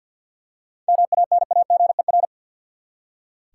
Morse code, heard as a keyed tone, about 47 wpm, 700 Hz. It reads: MANAGER
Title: Seven Letter Words - 50wpm